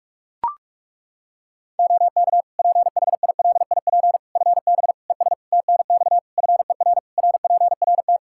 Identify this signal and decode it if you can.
Morse code, heard as a keyed tone, about 41 wpm, 700 Hz. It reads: OK PHILIP FB ES TNX FER RPRT
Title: K1USN SST QSO Practice 45wpm